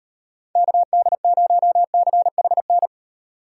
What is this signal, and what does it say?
Morse code, heard as a keyed tone, about 36 wpm, 700 Hz. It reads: KD0CHN